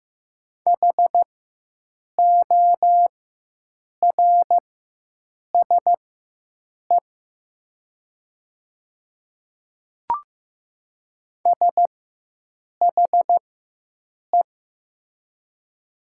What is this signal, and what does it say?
Morse code, heard as a keyed tone, about 15 wpm, spaced as for 7 wpm, 700 Hz. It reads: HORSE SHE
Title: 4x Character Spacing - Top 300 Words - 15wpm